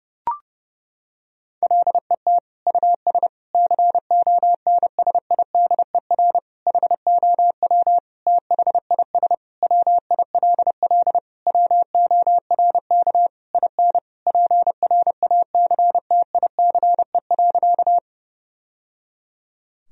Morse code, heard as a keyed tone, about 32 wpm, 700 Hz. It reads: LET US CONSIDER HOW THIS WILL WORK IN PRACTICE.
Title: Sentences from Top 1000 Words - Review Part 4 - 30wpm